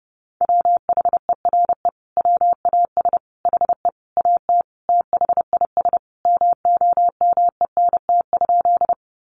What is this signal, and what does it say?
Morse code, heard as a keyed tone, about 30 wpm, 700 Hz. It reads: WHERE WAS HE AT THIS MOMENT?
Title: Sentences from Top 1000 Words 30wpm